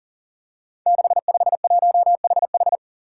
Morse code, heard as a keyed tone, about 42 wpm, 700 Hz. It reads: 651HH
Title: Five Challenging Letter-Number Combinations 40wpm